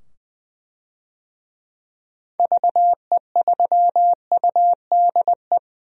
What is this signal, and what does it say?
Morse code, heard as a keyed tone, about 20 wpm, 700 Hz. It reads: VE3UDE
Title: DXCC-100 Call Signs - 6 Characters - 20wpm